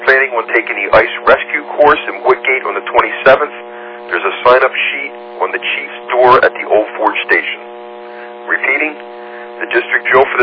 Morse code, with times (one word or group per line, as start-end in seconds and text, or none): none